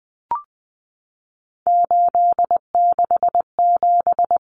Morse code, heard as a keyed tone, about 20 wpm, 700 Hz. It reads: 867
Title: Common Cut Numbers - 3-Digit 20wpm